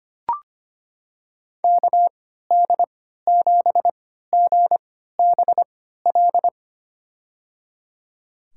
Morse code, heard as a keyed tone, about 25 wpm, spaced as for 14 wpm, 700 Hz. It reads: KD7GBL
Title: USA Call Signs - 6 Characters - 3x Character Spacing - 25wpm